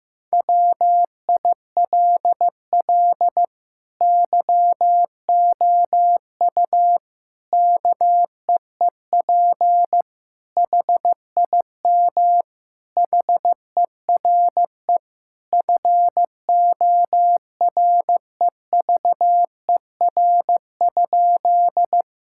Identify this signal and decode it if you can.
Morse code, heard as a keyed tone, about 15 wpm, 700 Hz. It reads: WILL YOU KEEP HIM HERE FOREVER?
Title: Sentences from Top 1600 Words 15wpm